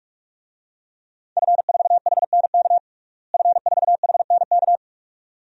Morse code, heard as a keyed tone, about 41 wpm, 700 Hz. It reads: F4HNK F4HNK
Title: DXCC-100 Call Signs - 5 Characters - Repeated 2x - 45wpm